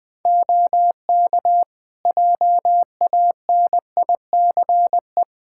Morse code, heard as a keyed tone, about 20 wpm, 700 Hz. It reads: OK JANICE
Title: Practice QSO - Exchange 3 - More Info - 20wpm